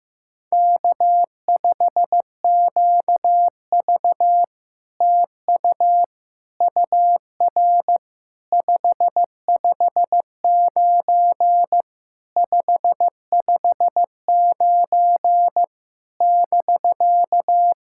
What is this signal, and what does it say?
Morse code, heard as a keyed tone, about 15 wpm, 700 Hz. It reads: K5QV TU UR 559 559 <BK>
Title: SOTA QSO Practice 15wpm